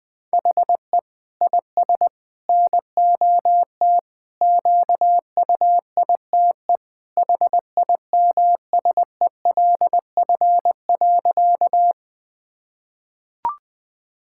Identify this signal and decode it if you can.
Morse code, heard as a keyed tone, about 20 wpm, 700 Hz. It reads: HE IS NOT QUITE HIMSELF.
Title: Sentences from Top 1600 Words 20wpm